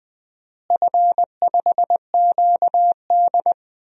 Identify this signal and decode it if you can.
Morse code, heard as a keyed tone, about 20 wpm, 700 Hz. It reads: F5QD
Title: DXCC-100 Call Signs - 4 Characters - Repeated 3x - 20wpm